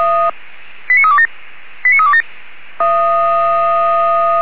Five-tone selective call (ZVEI), heard as quick runs of five tones, tones of about 70 ms. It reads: 89327 89327